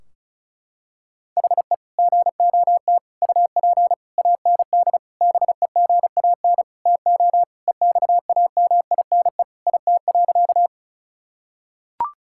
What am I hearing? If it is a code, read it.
Morse code, heard as a keyed tone, about 36 wpm, 700 Hz. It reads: HE GOT UP AND BEGAN TO EXAMINE IT.
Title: Sentences from Top 1600 Words 35wpm